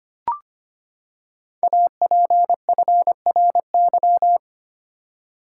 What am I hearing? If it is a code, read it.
Morse code, heard as a keyed tone, about 25 wpm, 700 Hz. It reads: APFRY
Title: Code Groups - 5 Letters - 25wpm